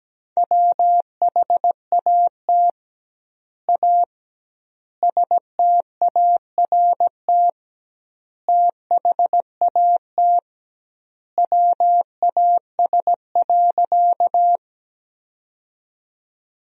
Morse code, heard as a keyed tone, about 17 wpm, 700 Hz. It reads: WHAT A START THAT WAS.
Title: 2x Word-Spacing - Sentences from Top 300 Words - 17wpm